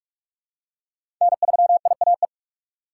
Morse code, heard as a keyed tone, about 47 wpm, 690 Hz. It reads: N3IAE
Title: USA Call Signs - 5 Characters - 2x Character Spacing - 45wpm